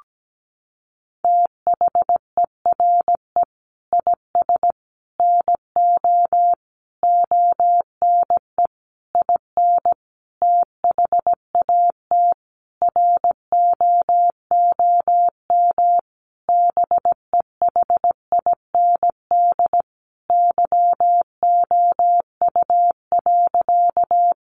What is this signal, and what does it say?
Morse code, heard as a keyed tone, about 17 wpm, 700 Hz. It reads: THERE IS NO ONE IN THAT ROOM BEHIND YOU.
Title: Sentences from Top 500 Words 17wpm